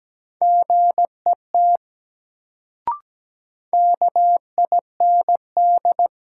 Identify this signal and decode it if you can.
Morse code, heard as a keyed tone, about 17 wpm, 700 Hz. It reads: GET KIND
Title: Sets of 2 Words - New Word Drill - 17wpm